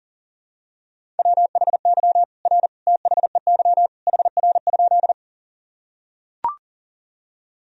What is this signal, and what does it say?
Morse code, heard as a keyed tone, about 40 wpm, 700 Hz. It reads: WHY R THEY HR?
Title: Sentences from Top 200 Words and Abbreviations 40wpm